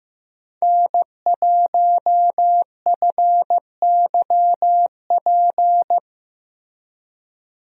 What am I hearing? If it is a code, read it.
Morse code, heard as a keyed tone, about 15 wpm, 700 Hz. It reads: N1FYP
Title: USA Call Signs - 5 Characters - Repeated 2x - 15wpm